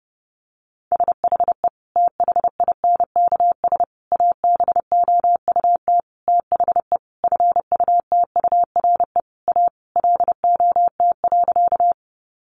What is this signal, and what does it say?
Morse code, heard as a keyed tone, about 30 wpm, 700 Hz. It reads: SHE THINKS ABOUT THE FUTURE A LOT.
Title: Easy Phrases 30wpm